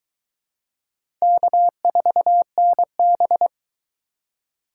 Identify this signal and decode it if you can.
Morse code, heard as a keyed tone, about 22 wpm, 700 Hz. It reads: K4NB